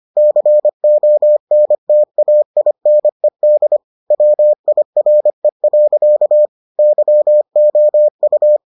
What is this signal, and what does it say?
Morse code, heard as a keyed tone, about 25 wpm, 600 Hz. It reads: CONTAINED WIRE. YOU